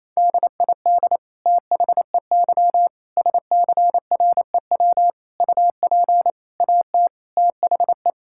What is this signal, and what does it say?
Morse code, heard as a keyed tone, about 28 wpm, 700 Hz. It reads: DID THEY SCREW UP AT THE
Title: Sentences from Top 1600 Words 28wpm